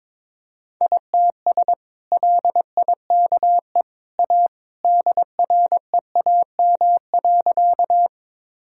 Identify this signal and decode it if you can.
Morse code, heard as a keyed tone, about 22 wpm, 700 Hz. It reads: ITS LIKE A DREAM.
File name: practice / 22wpm / Sentences with 2-Word Phrases - 22wpm